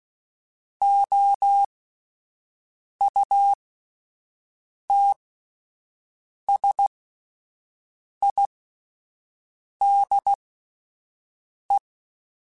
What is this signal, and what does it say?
Morse code, heard as a keyed tone, about 16 wpm, spaced as for 5 wpm, 800 Hz. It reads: OUTSIDE
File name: Spaced Characters / Spaced Characters-mp3s / Weather4 24 sp @10